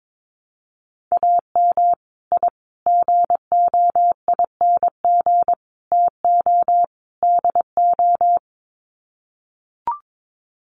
Morse code, heard as a keyed tone, about 22 wpm, 700 Hz. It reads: AM I GOING TO DO
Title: Sets of 5 Words Encore 22wpm